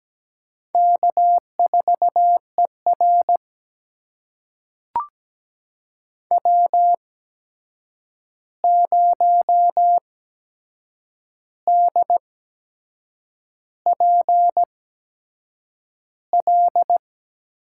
Morse code, heard as a keyed tone, about 17 wpm, spaced as for 7 wpm, 700 Hz. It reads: K4ER W0DPL
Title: USA Call Signs - 8x Character Spacing - 17wpm